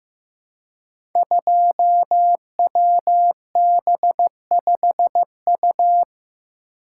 Morse code, heard as a keyed tone, about 15 wpm, 700 Hz. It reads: 2WB5U
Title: Code Groups - 5 Letter:Numbers - 15wpm